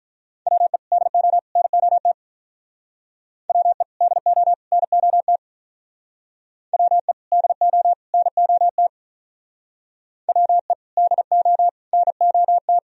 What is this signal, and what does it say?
Morse code, heard as a keyed tone, about 45 wpm, 700 Hz. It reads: WE DO NOT WE DO NOT WE DO NOT WE DO NOT